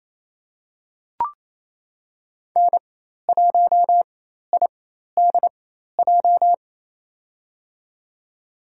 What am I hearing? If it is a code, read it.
Morse code, heard as a keyed tone, about 28 wpm, spaced as for 13 wpm, 700 Hz. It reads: N1IDJ
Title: USA Call Signs - 5 Characters - 4x Character Spacing - 28wpm